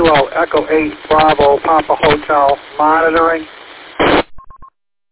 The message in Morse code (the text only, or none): none